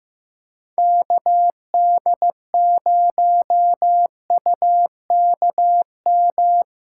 Morse code, heard as a keyed tone, about 18 wpm, 700 Hz. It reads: KD0UKM
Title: Speed Racing - USA 2x3 Call Signs 15wpm